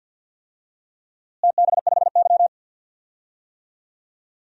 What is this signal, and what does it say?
Morse code, heard as a keyed tone, about 46 wpm, 700 Hz. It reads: TB5Y
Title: DXCC-100 Call Signs - 4 Characters - Repeated 3x - 50wpm